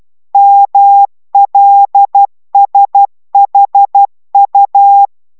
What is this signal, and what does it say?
Morse code, heard as a keyed tone, about 12 wpm, 800 Hz. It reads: MLSHU